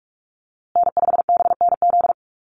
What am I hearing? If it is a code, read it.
Morse code, heard as a keyed tone, about 44 wpm, 690 Hz. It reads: N5BNZ
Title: USA Call Signs Repeated 3x 45wpm